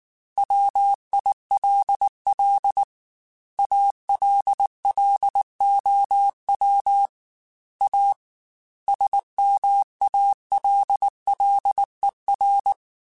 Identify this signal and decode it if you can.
Morse code, heard as a keyed tone, about 19 wpm, 800 Hz. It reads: WILL ALLOW A SMALLER
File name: Technical Subjects / Technical Subjects-mp3s / Sw Pwr Sup @18